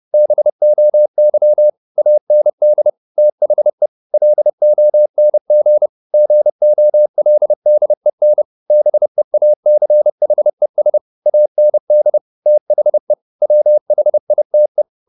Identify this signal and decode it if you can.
Morse code, heard as a keyed tone, about 30 wpm, 600 Hz. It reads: BOY AND THE LONG GOLDEN BEACHES AND THE WHITE
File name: the old man and the sea / the old man and the sea 1 30wpm